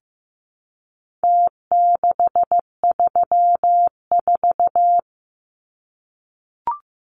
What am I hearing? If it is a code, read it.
Morse code, heard as a keyed tone, about 15 wpm, 700 Hz. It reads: T634
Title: Numbers - Contest Style - 4-Digit 15wpm